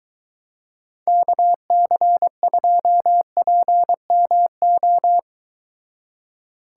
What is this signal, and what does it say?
Morse code, heard as a keyed tone, about 23 wpm, 700 Hz. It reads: KC2PMO